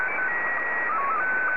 SSTV (slow-scan television), Scottie 1.